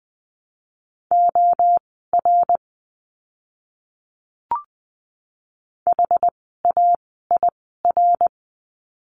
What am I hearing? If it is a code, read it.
Morse code, heard as a keyed tone, about 20 wpm, spaced as for 14 wpm, 700 Hz. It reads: OR HAIR
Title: 101 Lesson 17f - Letter u - Words 20wpm